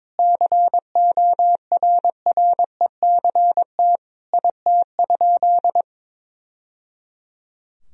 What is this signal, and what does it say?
Morse code, heard as a keyed tone, about 22 wpm, 700 Hz. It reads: CORRECT IT?
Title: Sentences from Top 1000 Words - Review Part 2 - 22wpm